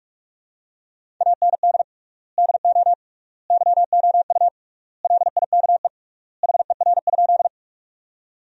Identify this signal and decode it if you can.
Morse code, heard as a keyed tone, about 41 wpm, 700 Hz. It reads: AND DO YOU LIKE HER?
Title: 3x Word-Spacing - Sentences from Top 100 Words - 45wpm